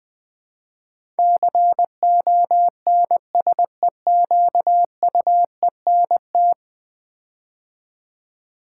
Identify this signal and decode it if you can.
Morse code, heard as a keyed tone, about 20 wpm, 700 Hz. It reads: CONSEQUENT
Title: Build A Word - 5 Letters At A Time - 12-Letter Words - 20wpm